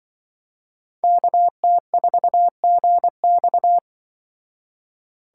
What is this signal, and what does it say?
Morse code, heard as a keyed tone, about 24 wpm, 700 Hz. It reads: KT4GX
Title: Speed Racing - USA 2x2 Call Signs 20wpm